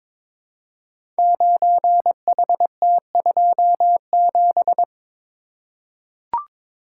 Morse code, heard as a keyed tone, about 22 wpm, 700 Hz. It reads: 9HT27
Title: Code Groups - 5 Letter:Numbers - 22wpm